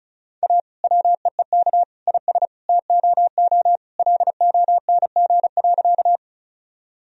Morse code, heard as a keyed tone, about 36 wpm, 700 Hz. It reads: A WEEK IS TOO LONG.
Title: Sentences from Top 1000 Words - Review Part 2 - 35wpm